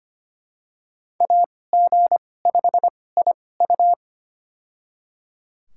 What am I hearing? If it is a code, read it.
Morse code, heard as a keyed tone, about 25 wpm, spaced as for 18 wpm, 700 Hz. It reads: AG5IU
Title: USA Call Signs - 5 Characters - 2x Character Spacing - 25wpm